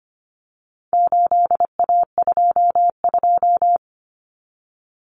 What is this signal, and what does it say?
Morse code, heard as a keyed tone, about 25 wpm, 700 Hz. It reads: 8A22
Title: Common Cut Numbers - 4-Digit 25wpm